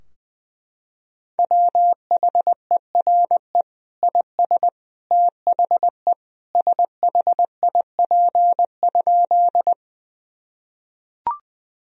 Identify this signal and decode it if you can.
Morse code, heard as a keyed tone, about 20 wpm, 700 Hz. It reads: WHERE IS THE SHIP?
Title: Sentences from Top 1000 Words - Review Part 2 - 20wpm